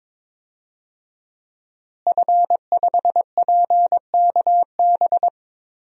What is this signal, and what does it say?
Morse code, heard as a keyed tone, about 22 wpm, 700 Hz. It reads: F5PKB
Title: DXCC-100 Call Signs - 5 Characters - Repeated 3x - 22wpm